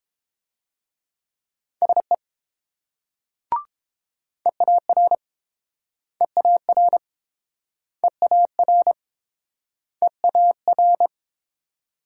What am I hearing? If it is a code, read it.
Morse code, heard as a keyed tone, about 29 wpm, 700 Hz. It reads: SE EAR EAR EAR EAR